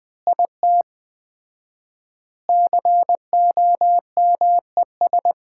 Morse code, heard as a keyed tone, about 20 wpm, 700 Hz. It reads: IT COMES
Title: 4x Word-Spacing - Sets of 5 Words - 20wpm